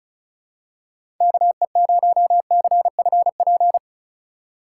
Morse code, heard as a keyed tone, about 36 wpm, 700 Hz. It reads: KE0CFP
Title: USA Call Signs - 2x Character Spacing - 35wpm